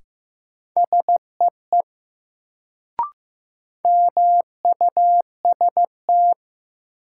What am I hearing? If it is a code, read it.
Morse code, heard as a keyed tone, about 15 wpm, 700 Hz. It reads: SEE MUST
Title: Sets of 2 Words - New Word Drill - 15wpm